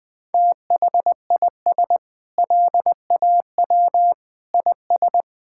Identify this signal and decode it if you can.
Morse code, heard as a keyed tone, about 21 wpm, 700 Hz. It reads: THIS LAW IS
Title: Sentences with 9-Word Phrases - 20wpm